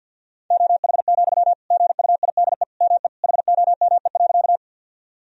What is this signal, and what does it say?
Morse code, heard as a keyed tone, about 50 wpm, 700 Hz. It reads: OH, GUIDE ME HOME.
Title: Sentences from Top 1000 Words - Review Part 3 - 50wpm